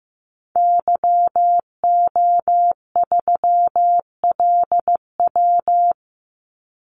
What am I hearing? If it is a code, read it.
Morse code, heard as a keyed tone, about 15 wpm, 700 Hz. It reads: YO3LW
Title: Call Signs Repeated 2x - 15wpm